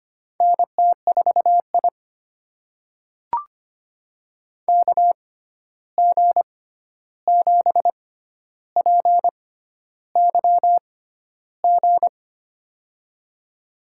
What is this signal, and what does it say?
Morse code, heard as a keyed tone, about 25 wpm, spaced as for 11 wpm, 700 Hz. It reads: NT4I KG7PYG
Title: USA Call Signs - 6x Character Spacing - 25wpm